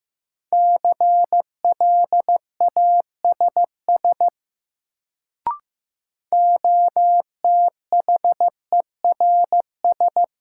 Morse code, heard as a keyed tone, about 15 wpm, 700 Hz. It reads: CLASS OTHERS
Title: Sentences from Top 1000 Words - Review Part 2 - New Word Drill - 15wpm